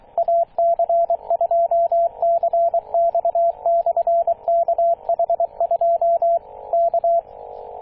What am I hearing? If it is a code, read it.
Morse code, heard as a keyed tone, about 23 wpm, 680 Hz, with radio noise. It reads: AC2CX/KH2 K
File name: posts / 2016-june-26-0520z-ac2cx-kh2-21-027mhz-cw